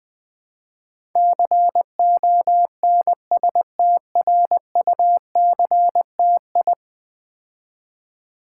Morse code, heard as a keyed tone, about 20 wpm, 700 Hz. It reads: CONSTRUCTI
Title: Build A Word - 5 Letters At A Time - 12-Letter Words - 20wpm